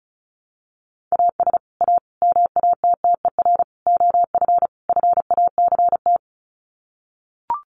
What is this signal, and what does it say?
Morse code, heard as a keyed tone, about 34 wpm, 700 Hz. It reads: AS A MATTER OF FACT
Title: Sets of 5 Words 35wpm